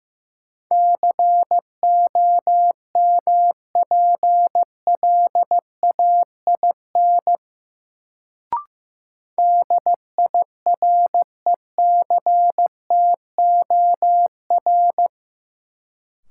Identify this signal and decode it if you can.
Morse code, heard as a keyed tone, about 15 wpm, 700 Hz. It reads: COMPLAIN DIRECTOR
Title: Sentences from Top 1300 Words - New Word Drill Alt - 15wpm